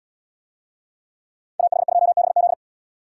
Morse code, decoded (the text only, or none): AH2BZ